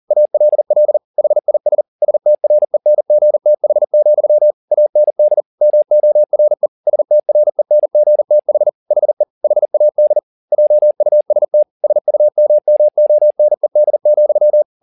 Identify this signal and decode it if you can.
Morse code, heard as a keyed tone, about 40 wpm, 600 Hz. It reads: ALL HIS STRENGTH, AND MORE STRENGTH HE HAD JUST SUMMONED,
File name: the old man and the sea 2 40wpm